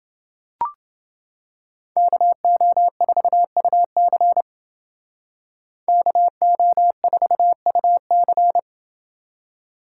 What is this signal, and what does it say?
Morse code, heard as a keyed tone, about 28 wpm, 700 Hz. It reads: KO4UC KO4UC